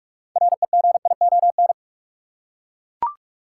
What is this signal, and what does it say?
Morse code, heard as a keyed tone, about 42 wpm, 700 Hz. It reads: REGION
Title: Six Letter Words - 45wpm